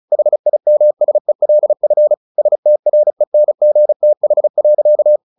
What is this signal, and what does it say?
Morse code, heard as a keyed tone, about 34 wpm, 600 Hz. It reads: HIMSELF STRENGTH.
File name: the old man and the sea / the old man and the sea 1 35wpm